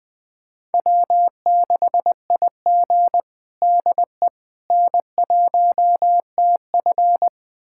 Morse code, heard as a keyed tone, about 20 wpm, 700 Hz. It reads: W6IG DE N1TF
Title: USA Callsign DE Callsign - 4 Characters - 20wpm